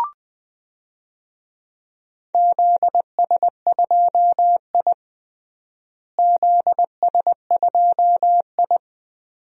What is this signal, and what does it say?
Morse code, heard as a keyed tone, about 20 wpm, 700 Hz. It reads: ZS2I ZS2I
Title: DXCC-100 Call Signs - 4 Characters - Repeated 2x - 20wpm